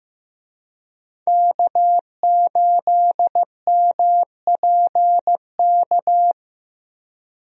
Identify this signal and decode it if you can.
Morse code, heard as a keyed tone, about 14 wpm, 700 Hz. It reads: K8MPK
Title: USA Call Signs - 5 Characters - Repeated 2x - 15wpm